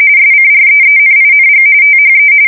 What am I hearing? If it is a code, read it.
RTTY, about 45.45 baud. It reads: SCREENING THE